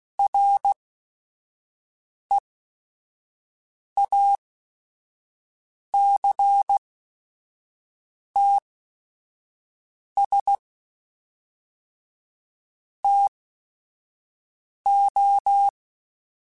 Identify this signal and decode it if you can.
Morse code, heard as a keyed tone, about 16 wpm, spaced as for 5 wpm, 800 Hz. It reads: REACTS TO
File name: Technical Subjects / Technical Subjects-mp3s / XC @5